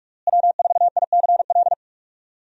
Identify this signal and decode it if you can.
Morse code, heard as a keyed tone, about 44 wpm, 700 Hz. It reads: W4ICL
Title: USA Call Signs - 5 Characters - Repeated 3x - 45wpm